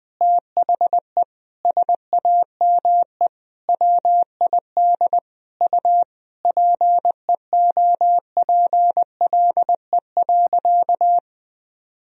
Morse code, heard as a keyed tone, about 20 wpm, 700 Hz. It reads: THE SAME WID U PEOPLE.
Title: Sentences from Top 300 Words and Abbreviations 20wpm